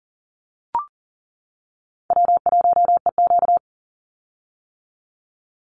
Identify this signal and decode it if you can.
Morse code, heard as a keyed tone, about 40 wpm, 700 Hz. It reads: W1EQ
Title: Easy Call Signs - 40wpm